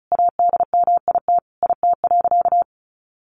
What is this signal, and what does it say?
Morse code, heard as a keyed tone, about 34 wpm, 700 Hz. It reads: ADMIT IT.
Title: A Sound Of Thunder 35wpm